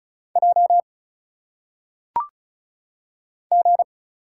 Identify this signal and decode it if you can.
Morse code, heard as a keyed tone, about 34 wpm, 700 Hz. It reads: J G
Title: Single Letters - Phonetic Pronunciation 35wpm